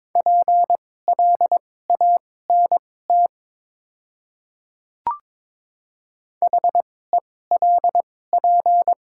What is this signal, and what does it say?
Morse code, heard as a keyed tone, about 22 wpm, spaced as for 16 wpm, 690 Hz. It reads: PLANT HELP
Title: 2x Character Spacing - Top 500 Words - 22wpm